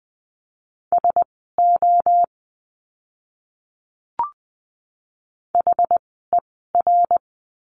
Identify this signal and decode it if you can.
Morse code, heard as a keyed tone, about 20 wpm, spaced as for 15 wpm, 700 Hz. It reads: SO HER
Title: 230 Lesson 37f - Letter x - Words 20wpm